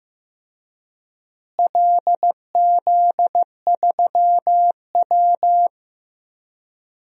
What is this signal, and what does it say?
Morse code, heard as a keyed tone, about 15 wpm, 700 Hz. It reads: LZ3W
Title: DXCC-100 Call Signs - 4 Characters - Repeated 2x - 15wpm